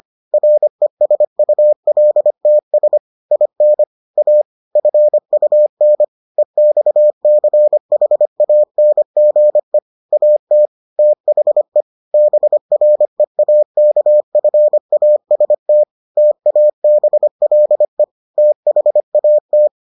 Morse code, heard as a keyed tone, about 25 wpm, 600 Hz. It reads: RESULTS IN A FUN EXCHANGE AT THE BREAKFAST TABLE THAT